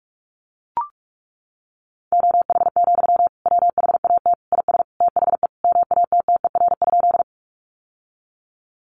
Morse code, heard as a keyed tone, about 44 wpm, 700 Hz. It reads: OH, WHAT IS THE MATTER?
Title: Sentences from Top 700 Words 45wpm